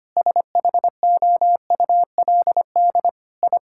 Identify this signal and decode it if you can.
Morse code, heard as a keyed tone, about 25 wpm, 700 Hz. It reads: SHOULD I
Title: Sentences from Top 1000 Words - Review Part 5 - 25wpm